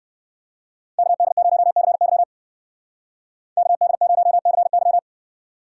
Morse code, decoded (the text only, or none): KD0CY KD0CY